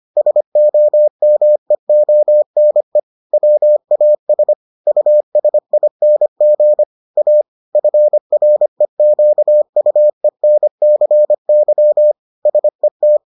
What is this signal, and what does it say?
Morse code, heard as a keyed tone, about 25 wpm, 600 Hz. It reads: SOMEONE WAS USING A FREQUENCY SET